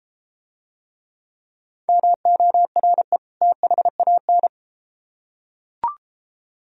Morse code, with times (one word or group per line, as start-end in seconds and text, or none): 1.9-3.2 MORE
3.4-4.5 THAN